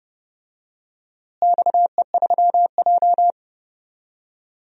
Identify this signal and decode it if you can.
Morse code, heard as a keyed tone, about 31 wpm, 700 Hz. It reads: XE3J
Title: DXCC-100 Call Signs - 4 Characters - Repeated 2x - 30wpm